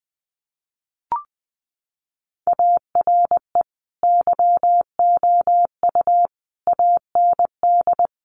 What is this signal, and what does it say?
Morse code, heard as a keyed tone, about 20 wpm, 700 Hz. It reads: ARE YOU AND
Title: Sentences from Top 1400 Words 20wpm